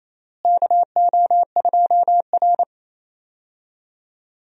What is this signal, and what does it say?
Morse code, heard as a keyed tone, about 28 wpm, 700 Hz. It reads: KO2R
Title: USA Call Signs - 4 Characters - Repeated 3x - 28wpm